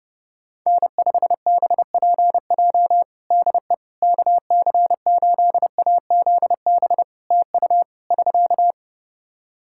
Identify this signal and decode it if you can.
Morse code, heard as a keyed tone, about 28 wpm, 700 Hz. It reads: N5BPJ DE KC8AZB TU <SK>
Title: K1USN SST QSO Practice 30wpm